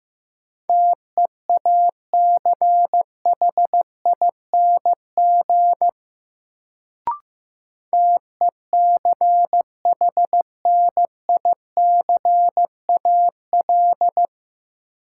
Morse code, heard as a keyed tone, about 15 wpm, 700 Hz. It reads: TEACHING TECHNICAL
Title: Sentences from Top 1100 Words - New Word Drill - 15wpm